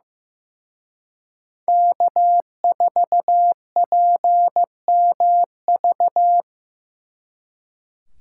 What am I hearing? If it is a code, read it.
Morse code, heard as a keyed tone, about 16 wpm, 700 Hz. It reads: K4PMV
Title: Speed Racing - USA 1x3 Call Signs 15wpm